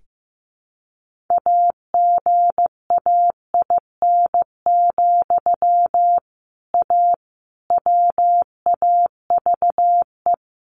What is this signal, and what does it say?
Morse code, heard as a keyed tone, about 15 wpm, 700 Hz. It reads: AGAIN, A WAVE
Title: Sentences from Top 1400 Words 15wpm